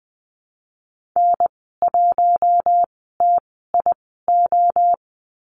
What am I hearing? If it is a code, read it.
Morse code, heard as a keyed tone, about 20 wpm, spaced as for 15 wpm, 700 Hz. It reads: N1TIO
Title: 048 Lesson 9g - Number 4 - Call Signs 20wpm